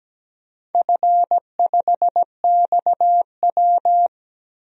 Morse code, heard as a keyed tone, about 17 wpm, 700 Hz. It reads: F5XW
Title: DXCC-100 Call Signs - 4 Characters - 17wpm